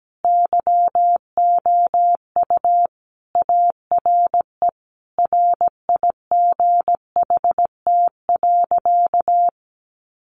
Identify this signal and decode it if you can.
Morse code, heard as a keyed tone, about 17 wpm, 700 Hz. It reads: YOU ARE RIGHT.
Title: Sentences from Top 300 Words 17wpm